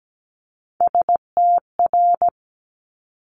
Morse code, heard as a keyed tone, about 17 wpm, 700 Hz. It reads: STR
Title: Multi-letter Phonemes Quick Fire 17wpm